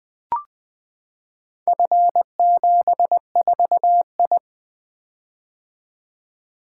Morse code, heard as a keyed tone, about 20 wpm, 700 Hz. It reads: F74I
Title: Four Letter-Number Combinations - 20wpm